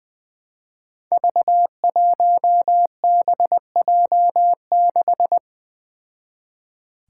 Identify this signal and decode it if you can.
Morse code, heard as a keyed tone, about 20 wpm, 700 Hz. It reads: V1BJ6
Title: Five Challenging Letter-Number Combinations 20wpm